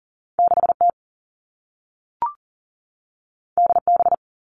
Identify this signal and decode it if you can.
Morse code, heard as a keyed tone, about 40 wpm, 700 Hz. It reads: 6T DB
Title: All Cut Numbers - 2-Digit 40wpm